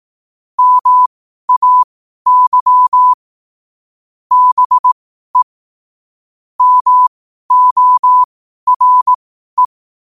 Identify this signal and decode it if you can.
Morse code, heard as a keyed tone, about 18 wpm, spaced as for 12 wpm, 1000 Hz. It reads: MAY BE MORE
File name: qotdfiles / 12wpm